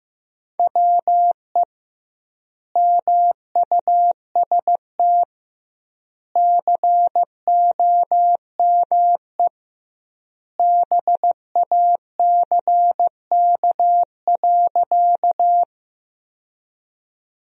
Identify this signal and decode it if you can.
Morse code, heard as a keyed tone, about 15 wpm, 700 Hz. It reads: WE MUST COME BACK.
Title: 2x Word-Spacing - Sentences from Top 200 Words - 15wpm